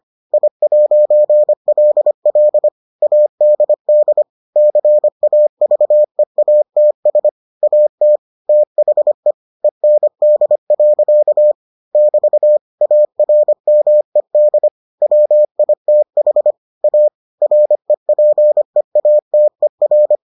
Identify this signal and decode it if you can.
Morse code, heard as a keyed tone, about 25 wpm, 600 Hz. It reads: I'LL ADD CAVEATS AT THE END. <BT> ARMED WITH A REPEATER